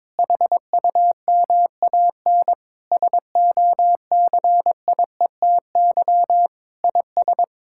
Morse code, heard as a keyed tone, about 22 wpm, 700 Hz. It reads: HUMAN SOCIETY IS
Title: Sentences from Top 1000 Words - Review Part 6 - 22wpm